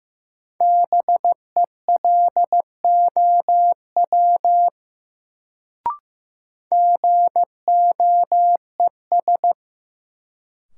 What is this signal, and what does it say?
Morse code, heard as a keyed tone, about 16 wpm, 700 Hz. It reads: BELOW GOES
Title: Sentences from Top 1000 Words - Review Part 5 - New Word Drill - 15wpm